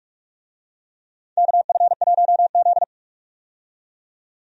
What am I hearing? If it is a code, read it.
Morse code, heard as a keyed tone, about 43 wpm, 700 Hz. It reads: KF1Z
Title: USA Call Signs - 4 Characters - 45wpm